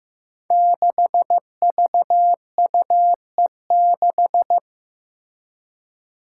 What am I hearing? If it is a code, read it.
Morse code, heard as a keyed tone, about 15 wpm, 700 Hz. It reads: 6VUE6
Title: Five Challenging Letter-Number Combinations 15wpm